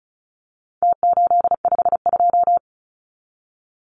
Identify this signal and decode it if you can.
Morse code, heard as a keyed tone, about 34 wpm, 700 Hz. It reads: T852
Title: Numbers - Contest Style - 4-Digit 35wpm